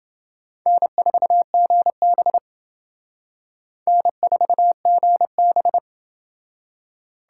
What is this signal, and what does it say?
Morse code, heard as a keyed tone, about 29 wpm, 700 Hz. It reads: N4GB N4GB